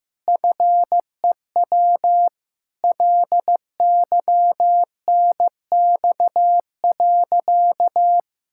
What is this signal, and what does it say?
Morse code, heard as a keyed tone, about 15 wpm, 700 Hz. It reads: FEW LYNX.
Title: Pangram Challenge v2 - 15wpm